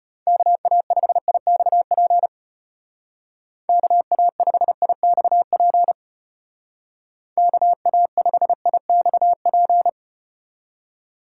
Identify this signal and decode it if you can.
Morse code, heard as a keyed tone, about 34 wpm, 700 Hz. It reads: KA5IXP KA5IXP KA5IXP